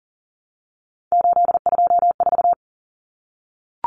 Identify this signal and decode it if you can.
Morse code, heard as a keyed tone, about 40 wpm, 700 Hz. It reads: 824
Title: Common Cut Numbers - 3-Digit 40wpm